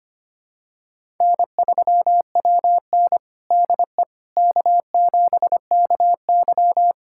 Morse code, heard as a keyed tone, about 25 wpm, 700 Hz. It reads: N3WN DE K7KY
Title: USA Callsign DE Callsign - 4 Characters - 25wpm